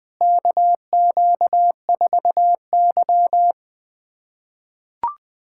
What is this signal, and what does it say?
Morse code, heard as a keyed tone, about 20 wpm, 700 Hz. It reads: KQ4Y
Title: USA Callsign DE Callsign - 4 Characters - 20wpm